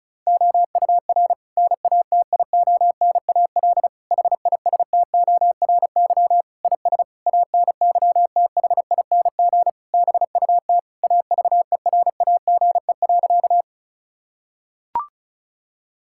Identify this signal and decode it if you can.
Morse code, heard as a keyed tone, about 36 wpm, 700 Hz. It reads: OUR NATIONAL HISTORY IS ANYTHING BUT AVERAGE.
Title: Sentences from Top 1000 Words - Review Part 6 - 35wpm